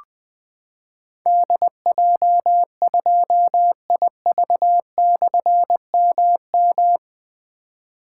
Morse code, heard as a keyed tone, about 20 wpm, 700 Hz. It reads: DJ2IV/MM
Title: Difficult Call Signs 20wpm